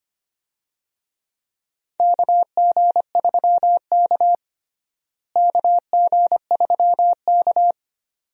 Morse code, heard as a keyed tone, about 25 wpm, 700 Hz. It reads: KG3K KG3K
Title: USA Call Signs - 4 Characters - Repeated 2x - 25wpm